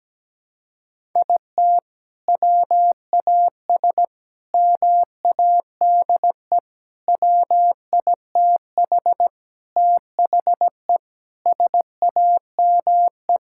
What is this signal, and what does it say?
Morse code, heard as a keyed tone, about 17 wpm, 700 Hz. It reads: IT WAS MADE WITH THE SAME
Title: Sentences from Top 1000 Words - Review Part 5 - 17wpm